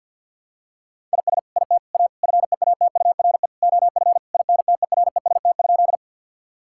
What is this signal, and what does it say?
Morse code, heard as a keyed tone, about 52 wpm, 700 Hz. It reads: IS IT A FEATURE OF INTEREST?
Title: Sentences from Top 1000 Words - Review Part 5 - 50wpm